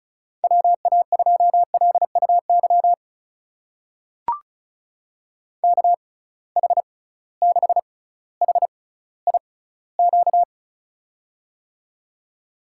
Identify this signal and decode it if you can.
Morse code, heard as a keyed tone, about 34 wpm, spaced as for 18 wpm, 700 Hz. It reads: WA2LUY KH6HIQ
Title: USA Call Signs - 6 Characters - 6x Character Spacing - 35wpm